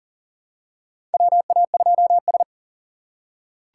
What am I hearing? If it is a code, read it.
Morse code, heard as a keyed tone, about 40 wpm, 700 Hz. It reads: WA2S
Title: USA Call Signs - 4 Characters - 4x Character Spacing - 40wpm